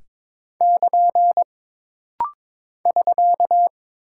Morse code, heard as a keyed tone, about 22 wpm, 700 Hz. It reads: <KN> <SK>
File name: practice / 22wpm / Single Punctuation-Prosign - Rapid-Fire - 22wpm